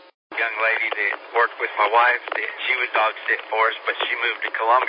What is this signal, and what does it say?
Emergency Alert System (SAME) tones, heard as harsh buzzing bursts, with speech. It is heard from 0.7 to 4.4 s.